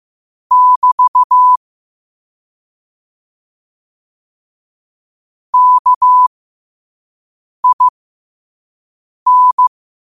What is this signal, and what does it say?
Morse code, heard as a keyed tone, about 15 wpm, spaced as for 5 wpm, 1000 Hz. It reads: <BT> KIN
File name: qotdfiles / 5wpm